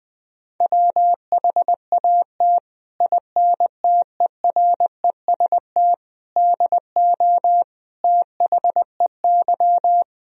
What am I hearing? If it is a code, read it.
Morse code, heard as a keyed tone, about 21 wpm, 700 Hz. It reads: WHAT INTEREST DO THEY
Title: Sentences from Top 1000 Words - Review Part 2 - 20wpm